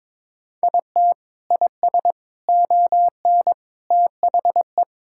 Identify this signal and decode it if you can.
Morse code, heard as a keyed tone, about 21 wpm, 700 Hz. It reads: IT IS ON THE
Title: Sentences from Top 1000 Words - Review Part 6 - 22wpm